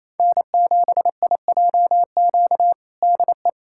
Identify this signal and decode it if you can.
Morse code, heard as a keyed tone, about 27 wpm, 700 Hz. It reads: N7IJQ DE
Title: USA Callsign DE Callsign 28wpm